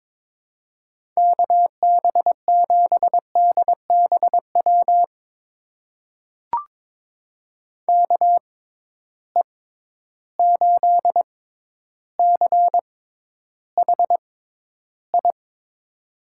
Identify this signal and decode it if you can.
Morse code, heard as a keyed tone, about 22 wpm, spaced as for 11 wpm, 700 Hz. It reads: KB7DBW KE8CHI
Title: USA Call Signs - 6 Characters - 6x Character Spacing - 22wpm